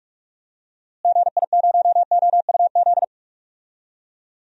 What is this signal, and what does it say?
Morse code, heard as a keyed tone, about 45 wpm, 700 Hz. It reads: GI0OUZ
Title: DXCC-100 Call Signs - 6 Characters - Repeated 3x - 45wpm